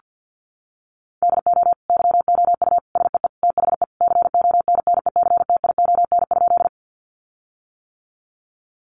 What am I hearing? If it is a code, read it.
Morse code, heard as a keyed tone, about 50 wpm, 700 Hz. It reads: DO YOU SEE THE CONNECTION?
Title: Sentences from Top 1300 Words 50wpm